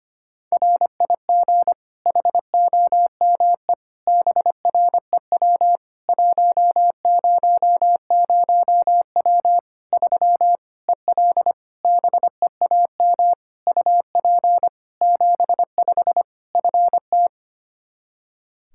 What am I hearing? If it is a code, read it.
Morse code, heard as a keyed tone, about 25 wpm, 700 Hz. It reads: RIG HOME BREW 100W 3 EL BEAM UP 75 FT
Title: K1USN SST QSO Practice 25wpm